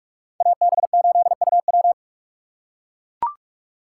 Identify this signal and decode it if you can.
Morse code, heard as a keyed tone, about 44 wpm, 700 Hz. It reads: AB8UW